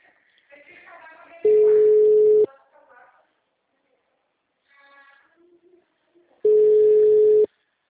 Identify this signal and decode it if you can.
Ringback tone (Europe, 425 Hz), with speech.
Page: Limeira (Frente)